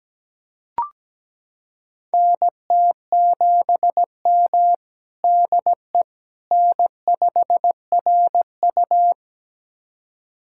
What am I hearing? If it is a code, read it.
Morse code, heard as a keyed tone, about 17 wpm, 700 Hz. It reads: NT7M DE N5RU
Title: USA Callsign DE Callsign - 4 Characters - 17wpm